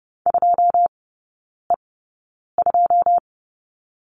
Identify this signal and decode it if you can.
Morse code, heard as a keyed tone, about 30 wpm, spaced as for 9 wpm, 700 Hz. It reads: 2E2
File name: 081 Lesson 14e - Number 2 - Three Characters 30wpm